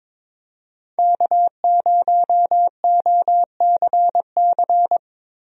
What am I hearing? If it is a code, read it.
Morse code, heard as a keyed tone, about 22 wpm, 700 Hz. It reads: K0OCC
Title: USA Call Signs - 5 Characters - Repeated 2x - 22wpm